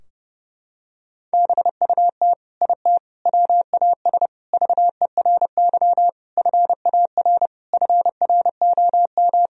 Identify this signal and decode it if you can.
Morse code, heard as a keyed tone, about 30 wpm, 700 Hz. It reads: BUT IT WAS VERY FAR FROM
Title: Sentences from Top 1000 Words - Review Part 2 - 30wpm